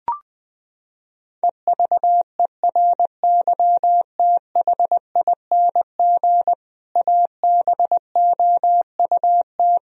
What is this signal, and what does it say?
Morse code, heard as a keyed tone, about 20 wpm, 700 Hz. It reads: EVERYTHING ABOUT
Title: Sentences from Top 1000 Words - Review Part 1 - 20wpm